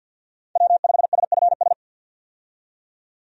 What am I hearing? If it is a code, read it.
Morse code, heard as a keyed tone, about 51 wpm, 700 Hz. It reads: W5SFS
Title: USA Call Signs - 5 Characters - 50wpm